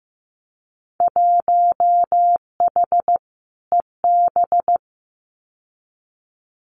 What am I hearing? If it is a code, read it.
Morse code, heard as a keyed tone, about 15 wpm, 700 Hz. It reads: 1H EB
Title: Winter Field Day Exchanges Encore 15wpm